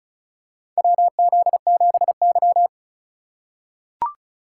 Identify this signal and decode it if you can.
Morse code, heard as a keyed tone, about 36 wpm, 700 Hz. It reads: WZ7Y
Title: USA Call Signs - 4 Characters - 4x Character Spacing - 35wpm